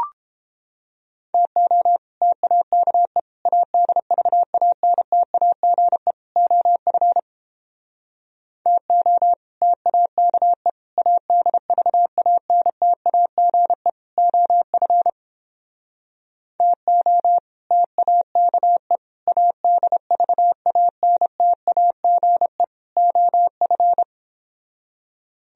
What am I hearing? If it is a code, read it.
Morse code, heard as a keyed tone, about 29 wpm, 700 Hz. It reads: TO TAKE ADVANTAGE OF TO TAKE ADVANTAGE OF TO TAKE ADVANTAGE OF